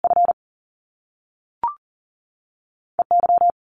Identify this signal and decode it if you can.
Morse code, heard as a keyed tone, about 38 wpm, 700 Hz. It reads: F E Y
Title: Top Two Letter Combinations in Words 40wpm